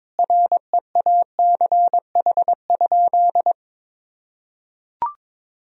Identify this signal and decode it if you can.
Morse code, heard as a keyed tone, about 22 wpm, 700 Hz. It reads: REACH?
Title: Sentences from Top 1000 Words - Review Part 4 - 22wpm